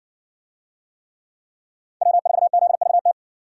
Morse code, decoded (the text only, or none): W4ZVT